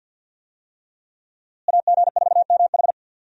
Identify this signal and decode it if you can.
Morse code, heard as a keyed tone, about 51 wpm, 700 Hz. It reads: AG4MH
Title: USA Call Signs - 5 Characters - Repeated 3x - 50wpm